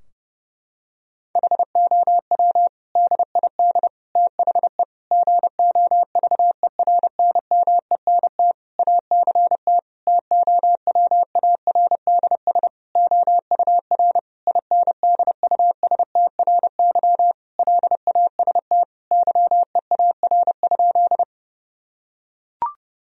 Morse code, heard as a keyed tone, about 32 wpm, 700 Hz. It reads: HOW DID THE GOVERNMENT ACT TOWARDS OUR INDUSTRY LAST YEAR?
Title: Sentences from Top 1000 Words - Review Part 4 - 30wpm